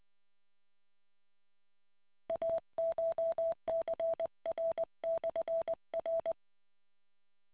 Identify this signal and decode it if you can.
Morse code, heard as a keyed tone, about 20 wpm, 660 Hz. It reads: ACHCR/R